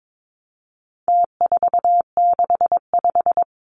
Morse code, heard as a keyed tone, about 22 wpm, 700 Hz. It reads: T465
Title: Numbers - Contest Style - 4-Digit 22wpm